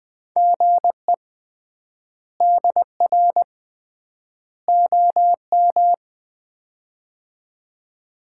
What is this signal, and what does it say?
Morse code, heard as a keyed tone, about 20 wpm, 700 Hz. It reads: GE DR OM
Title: Common QSO Phrases - 3x Word-Spacing - 20wpm